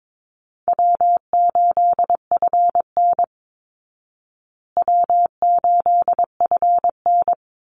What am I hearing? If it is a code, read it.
Morse code, heard as a keyed tone, about 22 wpm, 690 Hz. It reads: W8FN W8FN
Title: Easy Call Signs Repeated 2x - 22wpm